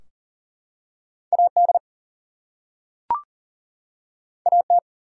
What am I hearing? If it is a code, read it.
Morse code, heard as a keyed tone, about 38 wpm, 700 Hz. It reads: AD AT
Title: Two Letter Words - 40wpm